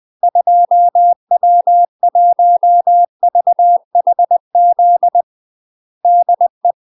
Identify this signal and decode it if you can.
Morse code, heard as a keyed tone, about 20 wpm, 700 Hz. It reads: 2W1VHZ DE
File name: UK MASTER QSOs TRAINING/20WPM